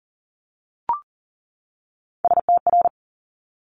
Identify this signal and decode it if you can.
Morse code, heard as a keyed tone, about 39 wpm, 700 Hz. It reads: STR
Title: Multi-letter Phonemes 40wpm